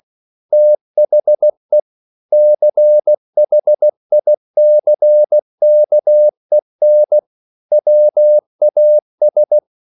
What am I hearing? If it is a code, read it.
Morse code, heard as a keyed tone, about 16 wpm, 600 Hz. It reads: THE CHICKEN WAS
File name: CW / Practice / EnglishQuotations0116